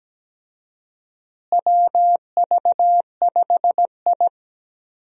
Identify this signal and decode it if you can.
Morse code, heard as a keyed tone, about 17 wpm, 700 Hz. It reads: WV5I
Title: USA Call Signs - 4 Characters - Repeated 2x - 17wpm